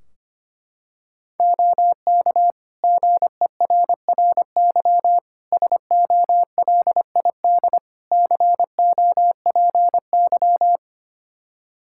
Morse code, heard as a keyed tone, about 25 wpm, 700 Hz. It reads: OK GERRY SOLID COPY
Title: Practice QSO - Exchange 3 - More Info - 25wpm